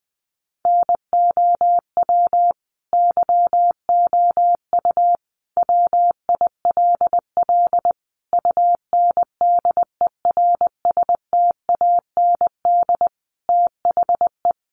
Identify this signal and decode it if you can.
Morse code, heard as a keyed tone, about 20 wpm, 700 Hz. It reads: NOW YOU WILL UNDERSTAND THE
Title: Sentences from Top 1200 Words 20wpm